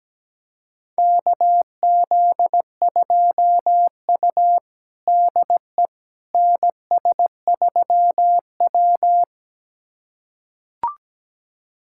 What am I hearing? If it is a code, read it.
Morse code, heard as a keyed tone, about 17 wpm, 700 Hz. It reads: KZ2U DE NS3W
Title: USA Callsign DE Callsign - 4 Characters - 17wpm